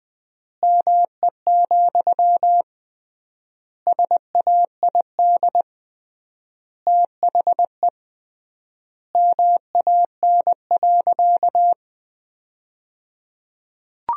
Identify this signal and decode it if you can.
Morse code, heard as a keyed tone, about 21 wpm, 700 Hz. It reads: ME, SAID THE MAN.
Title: 3x Word-Spacing - Sentences from Top 200 Words - 20wpm